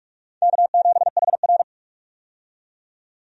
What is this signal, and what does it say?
Morse code, heard as a keyed tone, about 46 wpm, 700 Hz. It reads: K7HR
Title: USA Call Signs - 4 Characters - 45wpm